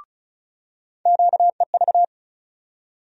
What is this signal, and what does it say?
Morse code, heard as a keyed tone, about 34 wpm, 700 Hz. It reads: QEV
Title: USA Call Sign Postfixes - 3 Letters 35wpm